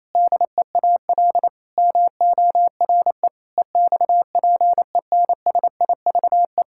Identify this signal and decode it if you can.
Morse code, heard as a keyed tone, about 28 wpm, 700 Hz. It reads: DEAL MORE EXPENSIVE
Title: Sentences from Top 1000 Words - Review Part 6 - 28wpm